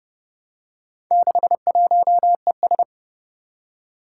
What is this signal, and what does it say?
Morse code, heard as a keyed tone, about 29 wpm, 700 Hz. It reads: 61ES
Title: Four Challenging Letter-Number Combinations 30wpm